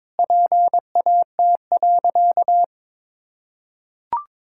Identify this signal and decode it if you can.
Morse code, heard as a keyed tone, about 21 wpm, 700 Hz. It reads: PAT.
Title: Pangram Challenge v2 - 22wpm